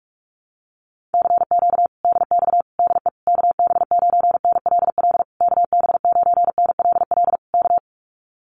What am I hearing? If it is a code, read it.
Morse code, heard as a keyed tone, about 44 wpm, 700 Hz. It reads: CQ DX DE KB9NLL KB9NLL K
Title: Common QSO Phrases Encore 45wpm